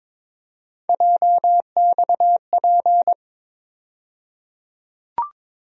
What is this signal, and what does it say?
Morse code, heard as a keyed tone, about 22 wpm, 700 Hz. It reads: JXP
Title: USA Call Sign Postfixes - 3 Letters 22wpm